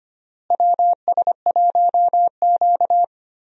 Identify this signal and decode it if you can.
Morse code, heard as a keyed tone, about 25 wpm, 700 Hz. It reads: WS1Q
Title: USA Call Signs - 4 Characters - Repeated 3x - 25wpm